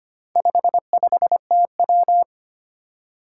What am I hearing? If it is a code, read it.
Morse code, heard as a keyed tone, about 25 wpm, 700 Hz. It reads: 55TW
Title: Four Challenging Letter-Number Combinations 25wpm